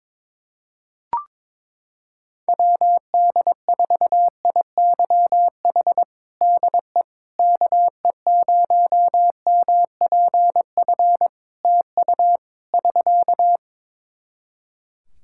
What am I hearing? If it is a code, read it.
Morse code, heard as a keyed tone, about 22 wpm, 700 Hz. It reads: WD4IYH DE KE0MPF TU <SK>
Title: Practice QSO - Exchange 4 - Ending - 22wpm